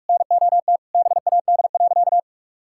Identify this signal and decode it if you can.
Morse code, heard as a keyed tone, about 46 wpm, 700 Hz. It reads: NOT BAD.
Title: Sentences from Top 1000 Words - Review Part 5 - 45wpm